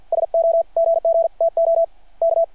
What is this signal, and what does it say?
Morse code, heard as a keyed tone, about 53 wpm, spaced as for 46 wpm, 660 Hz. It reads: SO GGTO K
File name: archived / CRSnet